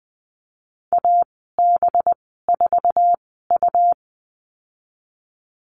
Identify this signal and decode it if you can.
Morse code, heard as a keyed tone, about 20 wpm, spaced as for 14 wpm, 700 Hz. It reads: AB4U
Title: 213 Lesson 34g - Number 8 - Call Signs 20wpm